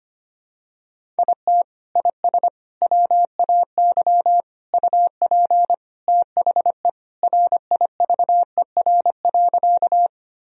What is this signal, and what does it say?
Morse code, heard as a keyed tone, about 25 wpm, 700 Hz. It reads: IT IS WAY UP THE RIVER.
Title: Sentences from Top 1000 Words - Review Part 1 - 25wpm